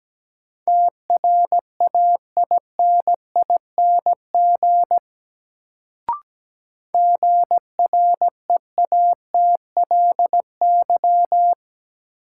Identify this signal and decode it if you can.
Morse code, heard as a keyed tone, about 17 wpm, 700 Hz. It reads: TRAINING GREATLY
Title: Sentences from Top 1400 Words - New Word Drill Alt - 17wpm